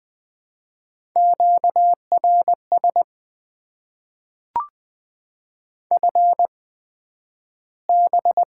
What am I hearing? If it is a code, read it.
Morse code, heard as a keyed tone, about 20 wpm, spaced as for 11 wpm, 700 Hz. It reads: QRS FB
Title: Top 90 QSO Elements - 8x Character Spacing - 20wpm